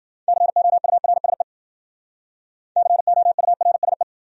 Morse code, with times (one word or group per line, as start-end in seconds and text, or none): 0.3-1.4 COURSE
2.8-4.0 COURSE